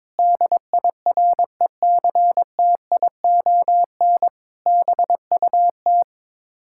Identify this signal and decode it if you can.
Morse code, heard as a keyed tone, about 22 wpm, 700 Hz. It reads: DIRECTION BUT
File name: practice / 22wpm / Sets of 8 Words - 22wpm